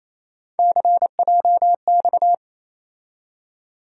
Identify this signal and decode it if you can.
Morse code, heard as a keyed tone, about 28 wpm, 700 Hz. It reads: CJX
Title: USA Call Sign Postfixes - 3 Letters 28wpm